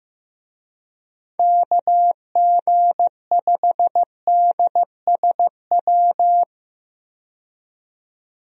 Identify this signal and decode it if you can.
Morse code, heard as a keyed tone, about 15 wpm, 700 Hz. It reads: KG5DSW
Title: USA Call Signs - 6 Characters - 15wpm